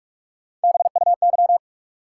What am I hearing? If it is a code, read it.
Morse code, heard as a keyed tone, about 46 wpm, 700 Hz. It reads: BUY
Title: Sentences from Top 700 Words - New Word Drill - 45wpm